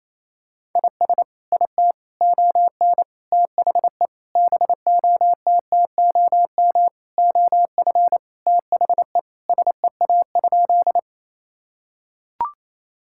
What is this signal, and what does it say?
Morse code, heard as a keyed tone, about 28 wpm, 700 Hz. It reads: IS IT ON THE BOTTOM OF THE SEA?
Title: Sentences from Top 1000 Words - Review Part 6 - 28wpm